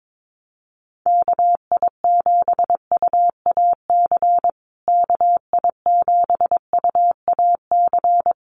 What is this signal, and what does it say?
Morse code, heard as a keyed tone, about 22 wpm, 700 Hz. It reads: KI7UAC KI7UAC
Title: USA Call Signs Repeated 3x 22wpm